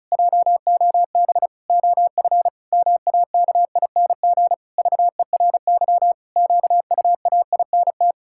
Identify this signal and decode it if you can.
Morse code, heard as a keyed tone, about 34 wpm, 700 Hz. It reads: JOB OF MAKING VERY QUAINT
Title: Pangram Challenge v2 - 35wpm